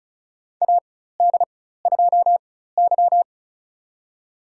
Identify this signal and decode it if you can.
Morse code, heard as a keyed tone, about 36 wpm, spaced as for 16 wpm, 700 Hz. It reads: AD2Y
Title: USA Call Signs - 4 Characters - 4x Character Spacing - 35wpm